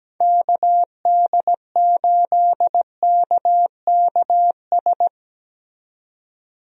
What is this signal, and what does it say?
Morse code, heard as a keyed tone, about 17 wpm, 700 Hz. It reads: KD8KKS
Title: USA Call Signs - 6 Characters - 17wpm